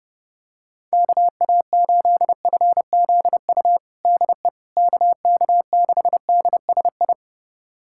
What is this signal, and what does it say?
Morse code, heard as a keyed tone, about 30 wpm, 700 Hz. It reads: KA8FZU DE KK6DSI
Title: USA Callsign DE Callsign - 6 Characters - 30wpm